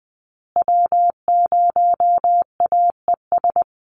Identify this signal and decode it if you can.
Morse code, heard as a keyed tone, about 20 wpm, 700 Hz. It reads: W0AES
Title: USA Call Signs Repeated 3x 20wpm